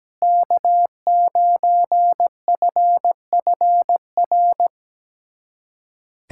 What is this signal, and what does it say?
Morse code, heard as a keyed tone, about 17 wpm, 700 Hz. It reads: K9FFR
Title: USA Callsign DE Callsign - 5 Characters - 17wpm